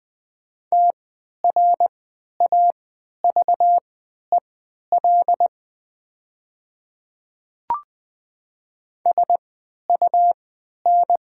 Morse code, heard as a keyed tone, about 20 wpm, spaced as for 11 wpm, 700 Hz. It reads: TRAVEL SUN
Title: 3x Character Spacing - Top 500 Words - 20wpm